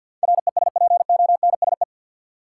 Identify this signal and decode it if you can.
Morse code, heard as a keyed tone, about 48 wpm, 700 Hz. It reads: RESPONSE
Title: Sentences from Top 1400 Words - New Word Drill - 50wpm